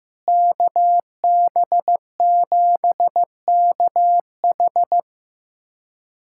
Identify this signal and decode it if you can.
Morse code, heard as a keyed tone, about 15 wpm, 700 Hz. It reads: KB7KH
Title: USA Callsign DE Callsign 15wpm